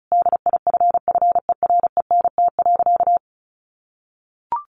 Morse code, heard as a keyed tone, about 34 wpm, 700 Hz. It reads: DIFFERENT.
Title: Sentences from Top 1300 Words 35wpm